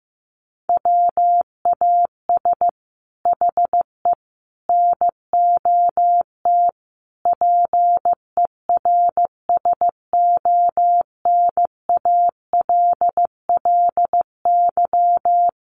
Morse code, heard as a keyed tone, about 15 wpm, 700 Hz. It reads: WAS HE NOT PERSONALLY
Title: Sentences from Top 1400 Words 15wpm